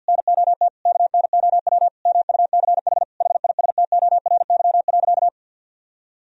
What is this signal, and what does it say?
Morse code, heard as a keyed tone, about 53 wpm, 700 Hz. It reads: NOT KNOW MUCH HISTORY.
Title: Sentences from Top 1000 Words - Review Part 2 - 50wpm